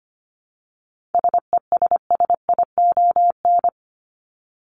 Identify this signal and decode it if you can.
Morse code, heard as a keyed tone, about 25 wpm, 700 Hz. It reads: SESSION
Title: Top 1501-1750 Words 25wpm